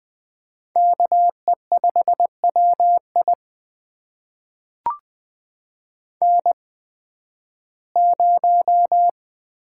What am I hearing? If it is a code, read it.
Morse code, heard as a keyed tone, about 20 wpm, spaced as for 13 wpm, 700 Hz. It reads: KE5WI N0
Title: USA Call Signs - 5 Characters - 8x Character Spacing - 20wpm